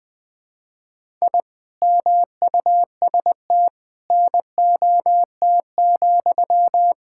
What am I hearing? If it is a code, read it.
Morse code, heard as a keyed tone, about 21 wpm, 700 Hz. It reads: I MUST NOT,
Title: Sentences from Top 1000 Words - Review Part 5 - 20wpm